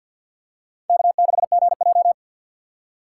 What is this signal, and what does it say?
Morse code, heard as a keyed tone, about 51 wpm, 700 Hz. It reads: K6GJ
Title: USA Call Signs - 4x Character Spacing - 50wpm